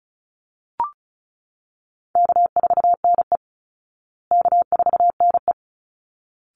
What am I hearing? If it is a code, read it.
Morse code, heard as a keyed tone, about 34 wpm, 700 Hz. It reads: K4NE K4NE
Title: Call Signs Repeated 3x - 35wpm